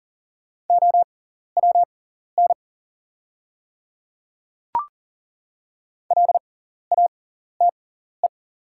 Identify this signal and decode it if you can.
Morse code, heard as a keyed tone, about 40 wpm, spaced as for 14 wpm, 690 Hz. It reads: OWN LATE
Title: 6x Character Spacing - Top 300 Words - 40wpm